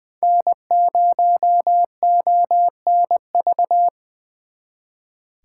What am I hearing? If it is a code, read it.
Morse code, heard as a keyed tone, about 23 wpm, 700 Hz. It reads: N0ONV
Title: Speed Racing - USA 1x3 Call Signs 20wpm